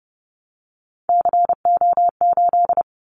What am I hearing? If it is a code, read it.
Morse code, heard as a keyed tone, about 28 wpm, 700 Hz. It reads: CO8
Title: Callsign Prefixes for Top 100 DXCC - 30wpm